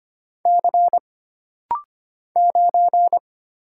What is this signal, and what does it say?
Morse code, heard as a keyed tone, about 25 wpm, 700 Hz. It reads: C 9
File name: Single Character (Letter:Number:Punctuation:Prosign) - Rapid-Fire - 25wpm